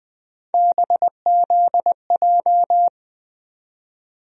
Morse code, heard as a keyed tone, about 20 wpm, 700 Hz. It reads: BZJ
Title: USA Call Sign Postfixes - 3 Letters 20wpm